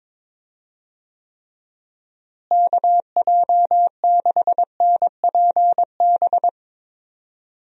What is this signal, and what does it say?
Morse code, heard as a keyed tone, about 22 wpm, 700 Hz. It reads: KJ6NPB
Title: USA Call Signs - 6 Characters - Repeated 3x - 22wpm